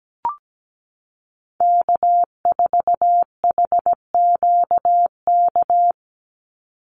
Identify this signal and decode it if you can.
Morse code, heard as a keyed tone, about 17 wpm, 700 Hz. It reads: K4HQK
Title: Call Signs Repeated 2x - 17wpm